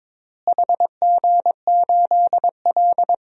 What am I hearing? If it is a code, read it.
Morse code, heard as a keyed tone, about 22 wpm, 700 Hz. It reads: HG8L
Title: DXCC-100 Call Signs - 4 Characters - Repeated 3x - 22wpm